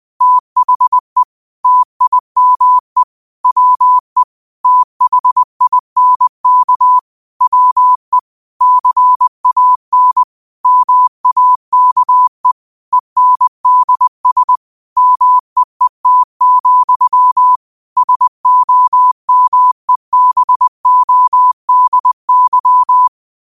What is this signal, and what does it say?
Morse code, heard as a keyed tone, about 20 wpm, 1000 Hz. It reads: THE TIME WE THINK WE CAN MAKE ENDS MEET, SOMEBODY
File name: qotdfiles / 20wpm